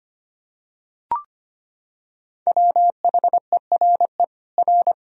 Morse code, heard as a keyed tone, about 25 wpm, 700 Hz. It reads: WHERE R
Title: Sentences from Top 200 Words and Abbreviations 25wpm